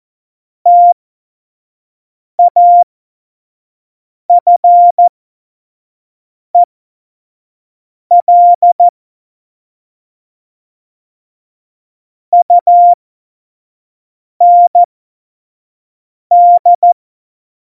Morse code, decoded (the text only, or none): TAFEL UND